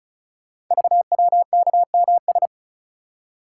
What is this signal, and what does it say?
Morse code, heard as a keyed tone, about 36 wpm, 700 Hz. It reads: VWKMS
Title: Code Groups - 5 Letters - 35wpm